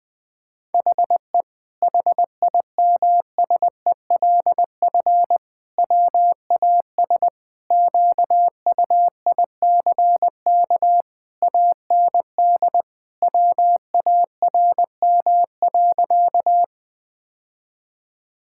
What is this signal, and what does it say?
Morse code, heard as a keyed tone, about 20 wpm, 700 Hz. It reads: HE HIMSELF WAS QUICK AND WARM.
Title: Sentences from Top 1600 Words 20wpm